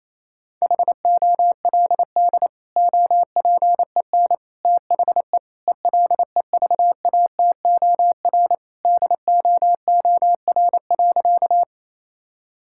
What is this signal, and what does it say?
Morse code, heard as a keyed tone, about 28 wpm, 700 Hz. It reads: HOLD OPEN THE ELEVATOR DOOR.
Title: Sentences from Top 1600 Words 28wpm